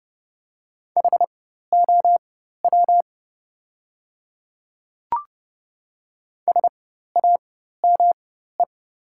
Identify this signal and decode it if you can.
Morse code, heard as a keyed tone, about 30 wpm, spaced as for 14 wpm, 700 Hz. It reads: HOW SAME
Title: 4x Character Spacing - Top 200 Words - 30wpm